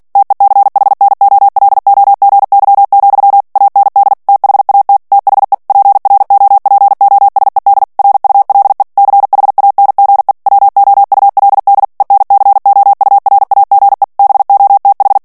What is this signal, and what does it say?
Morse code, heard as a keyed tone, about 48 wpm, 800 Hz. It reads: TECHNOLOGY, AND THAT THE PROPOSED RULE CHANGE WOULD ENCOURAGE BOTH
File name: ANGLAIS texte clair 3 vitesse 45 mots minute